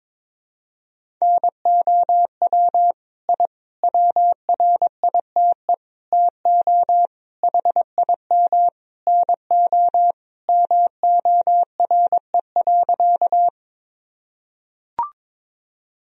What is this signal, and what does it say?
Morse code, heard as a keyed tone, about 21 wpm, 700 Hz. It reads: NOW I WRITE TO HIM NO MORE.
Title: Sentences from Top 1000 Words - Review Part 1 - 22wpm